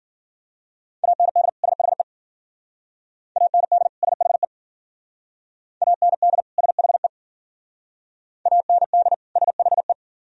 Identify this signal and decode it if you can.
Morse code, heard as a keyed tone, about 52 wpm, 700 Hz. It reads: AND SHE AND SHE AND SHE AND SHE